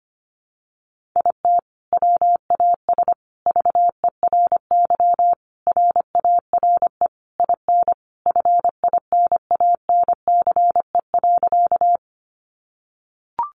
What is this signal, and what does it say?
Morse code, heard as a keyed tone, about 25 wpm, 700 Hz. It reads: IT WAS VERY RARE IN FINANCE.
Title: Sentences from Top 1500 Words 25wpm